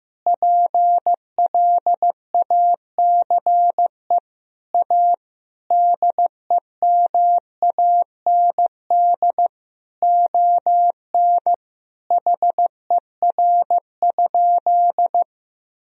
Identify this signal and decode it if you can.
Morse code, heard as a keyed tone, about 15 wpm, 700 Hz. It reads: PLACE A DEMAND ON HER?
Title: Sentences from Top 1000 Words - Review Part 4 - 15wpm